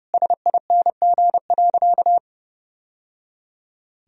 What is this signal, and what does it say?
Morse code, heard as a keyed tone, about 30 wpm, 700 Hz. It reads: SING.
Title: Sentences from Top 1000 Words - Review Part 2 - 30wpm